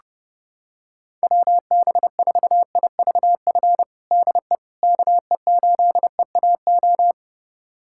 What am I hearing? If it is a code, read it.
Morse code, heard as a keyed tone, about 30 wpm, 700 Hz. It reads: WB4IVF DE KE8EAO
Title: USA Callsign DE Callsign 30wpm